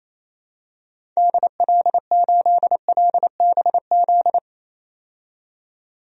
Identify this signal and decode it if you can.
Morse code, heard as a keyed tone, about 28 wpm, 700 Hz. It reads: DL8LBZ
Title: DXCC-100 Call Signs - 6 Characters - 28wpm